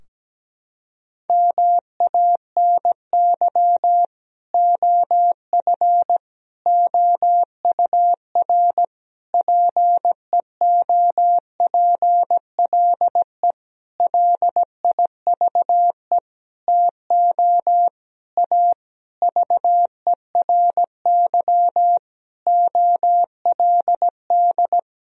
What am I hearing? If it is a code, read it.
Morse code, heard as a keyed tone, about 17 wpm, 700 Hz. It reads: MANY OF OUR PEOPLE LIVE TO A VERY OLD
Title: Sentences from Top 1000 Words - Review Part 3 - 17wpm